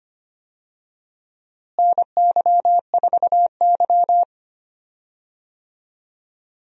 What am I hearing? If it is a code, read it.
Morse code, heard as a keyed tone, about 25 wpm, 700 Hz. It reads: NY4Y
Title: USA Call Signs - 4 Characters - Repeated 2x - 25wpm